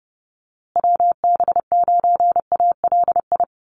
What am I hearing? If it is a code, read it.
Morse code, heard as a keyed tone, about 29 wpm, 700 Hz. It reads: WB9ALI
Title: USA Call Signs Repeated 3x 30wpm